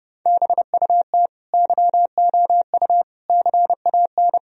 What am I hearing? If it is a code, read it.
Morse code, heard as a keyed tone, about 32 wpm, 700 Hz. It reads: BUT YOU CAN
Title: Sentences from Top 1000 Words - Review Part 4 - 30wpm